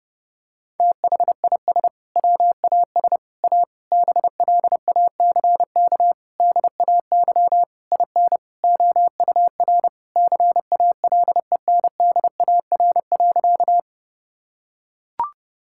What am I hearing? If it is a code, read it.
Morse code, heard as a keyed tone, about 30 wpm, 700 Hz. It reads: THIS WAS A BLACK DAY IN OUR CALENDAR.
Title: Sentences from Top 1600 Words 30wpm